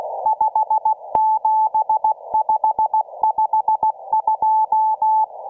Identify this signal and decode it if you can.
Morse code, heard as a keyed tone, about 16 wpm, 870 Hz. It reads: 57552